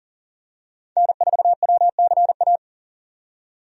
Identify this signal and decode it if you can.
Morse code, heard as a keyed tone, about 40 wpm, 700 Hz. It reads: N4WCA
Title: USA Call Signs - 5 Characters - 40wpm